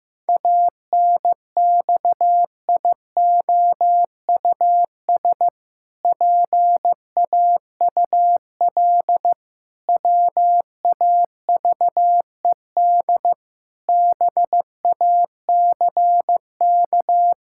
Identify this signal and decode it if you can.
Morse code, heard as a keyed tone, about 15 wpm, 700 Hz. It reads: ANXIOUS PAUL WAVED BACK
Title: Pangram Challenge v2 - 15wpm